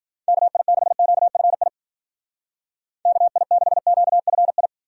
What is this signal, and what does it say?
Morse code, heard as a keyed tone, about 49 wpm, 700 Hz. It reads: KI6QFI KI6QFI
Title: Speed Racing - USA 2x3 Call Signs 40wpm